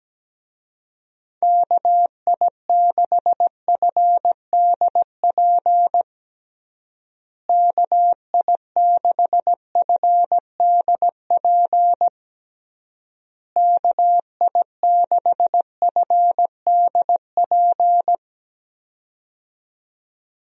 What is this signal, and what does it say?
Morse code, heard as a keyed tone, about 17 wpm, 700 Hz. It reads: KI6FDP KI6FDP KI6FDP
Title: USA Call Signs - 6 Characters - Repeated 3x - 17wpm